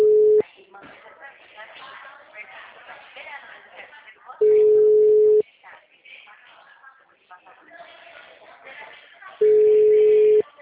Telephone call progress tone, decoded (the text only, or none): ringback tone